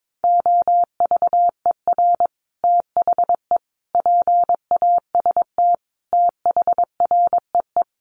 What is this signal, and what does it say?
Morse code, heard as a keyed tone, about 22 wpm, 700 Hz. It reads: OVER THE PAST THREE
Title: Sets of 5 Words Encore2 22wpm